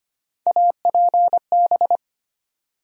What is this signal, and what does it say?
Morse code, heard as a keyed tone, about 25 wpm, 700 Hz. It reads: APB
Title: USA Call Sign Postfixes - 3 Letters 25wpm